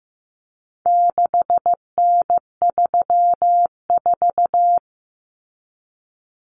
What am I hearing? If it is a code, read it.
Morse code, heard as a keyed tone, about 15 wpm, 700 Hz. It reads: 6N34
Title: Common Cut Numbers - 4-Digit 15wpm